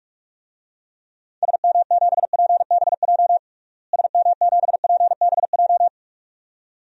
Morse code, heard as a keyed tone, about 47 wpm, 700 Hz. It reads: SM7PBJ SM7PBJ
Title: DXCC-100 Call Signs - 6 Characters - Repeated 2x - 45wpm